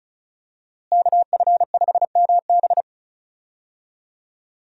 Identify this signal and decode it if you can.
Morse code, heard as a keyed tone, about 36 wpm, 700 Hz. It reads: KF5MB
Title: USA Call Signs - 5 Characters - Repeated 2x - 35wpm